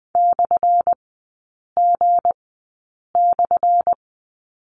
Morse code, heard as a keyed tone, about 20 wpm, spaced as for 8 wpm, 700 Hz. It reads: /G/
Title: 178 Lesson 29e - Slash - Three Characters 20wpm